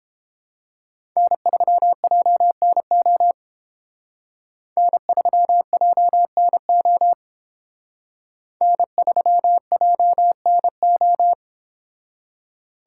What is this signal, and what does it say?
Morse code, heard as a keyed tone, about 29 wpm, 700 Hz. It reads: N3JNO N3JNO N3JNO